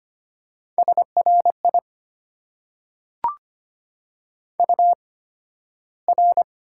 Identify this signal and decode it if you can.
Morse code, heard as a keyed tone, about 25 wpm, spaced as for 13 wpm, 690 Hz. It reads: SRI UR
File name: Top 90 QSO Elements - 8x Character Spacing - 25wpm